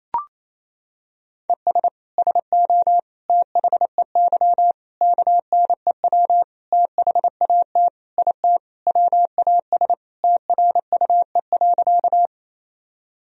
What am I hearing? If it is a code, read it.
Morse code, heard as a keyed tone, about 27 wpm, 700 Hz. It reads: ES SO THEY KNEW THAT IT WAS TRUE.
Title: Sentences from Top 400 Words and Abbreviations 28wpm